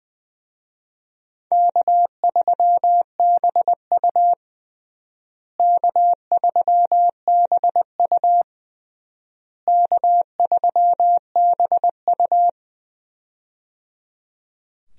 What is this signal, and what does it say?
Morse code, heard as a keyed tone, about 20 wpm, 700 Hz. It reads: K3BU K3BU K3BU
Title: USA Call Signs - 4 Characters - Repeated 3x - 20wpm